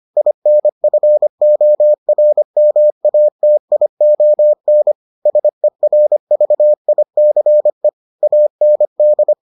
Morse code, heard as a keyed tone, about 25 wpm, 600 Hz. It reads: INFORMATION SERVICE AND